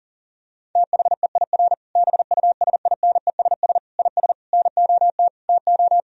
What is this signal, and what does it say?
Morse code, heard as a keyed tone, about 40 wpm, 700 Hz. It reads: THEIR BUSINESS IS NOT TO
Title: Sentences from Top 1000 Words - Review Part 3 - 40wpm